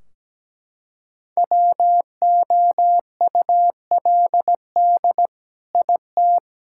Morse code, heard as a keyed tone, about 17 wpm, 700 Hz. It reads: WOULD IT
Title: Sentences from Top 1600 Words 17wpm